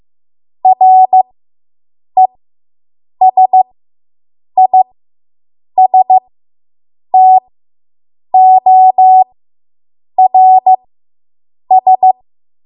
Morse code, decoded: RESISTORS